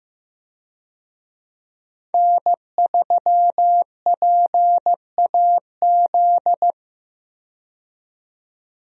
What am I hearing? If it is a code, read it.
Morse code, heard as a keyed tone, about 15 wpm, 700 Hz. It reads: N3PAZ
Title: USA Call Signs - 5 Characters - 15wpm